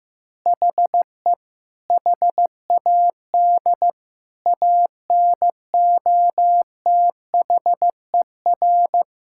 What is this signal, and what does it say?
Morse code, heard as a keyed tone, about 16 wpm, 700 Hz. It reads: HE HAD ANOTHER
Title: Sentences from Top 1000 Words - Review Part 6 - 15wpm